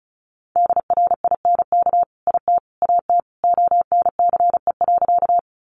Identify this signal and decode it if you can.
Morse code, heard as a keyed tone, about 34 wpm, 700 Hz. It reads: DRINK IT AT ONCE.
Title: Sentences from Top 1500 Words 35wpm